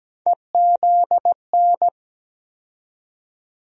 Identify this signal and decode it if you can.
Morse code, heard as a keyed tone, about 17 wpm, 700 Hz. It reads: EZN
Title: USA Call Sign Postfixes - 3 Letters 17wpm